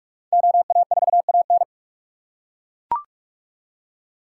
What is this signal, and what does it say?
Morse code, heard as a keyed tone, about 42 wpm, 700 Hz. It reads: OA4AN